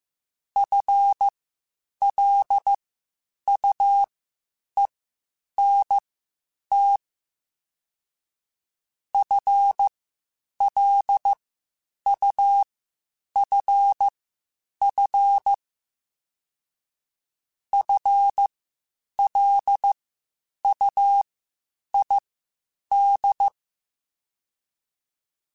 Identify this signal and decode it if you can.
Morse code, heard as a keyed tone, about 15 wpm, spaced as for 8 wpm, 800 Hz. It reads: FLUENT FLUFF FLUID